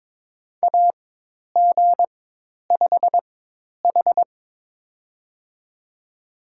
Morse code, heard as a keyed tone, about 22 wpm, spaced as for 10 wpm, 700 Hz. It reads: AG5H
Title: USA Call Signs - 4 Characters - 4x Character Spacing - 22wpm